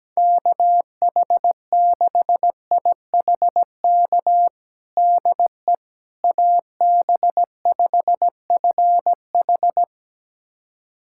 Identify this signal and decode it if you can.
Morse code, heard as a keyed tone, about 17 wpm, 700 Hz. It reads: KH6IHK DE AB5FH
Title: USA Callsign DE Callsign 17wpm